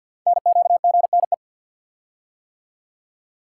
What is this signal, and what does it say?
Morse code, heard as a keyed tone, about 52 wpm, 700 Hz. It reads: NQGNE
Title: Code Groups - 5 Letters - 50wpm